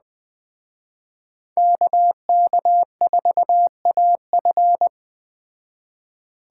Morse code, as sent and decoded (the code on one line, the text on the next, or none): -.- -.- ....- .- ..-.
KK4AF